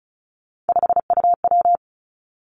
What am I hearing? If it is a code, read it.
Morse code, heard as a keyed tone, about 36 wpm, 700 Hz. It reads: 5UW
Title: Three Challenging Letter-Number Combinations 35wpm